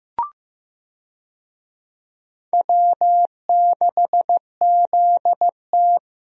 Morse code, heard as a keyed tone, about 15 wpm, 700 Hz. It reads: W6ZT
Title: USA Call Signs - 4 Characters - Repeated 2x - 15wpm